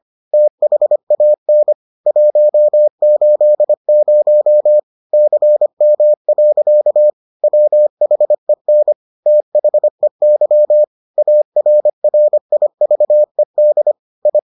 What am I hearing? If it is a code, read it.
Morse code, heard as a keyed tone, about 25 wpm, 600 Hz. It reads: THAN 180 CM. WHEN THEY ARRIVED I